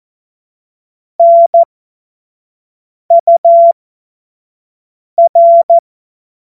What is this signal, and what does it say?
Morse code, heard as a keyed tone, about 14 wpm, spaced as for 5 wpm, 680 Hz. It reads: NUR
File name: Morsekurs / Witze / Witz 17. Januar 2023 25 F70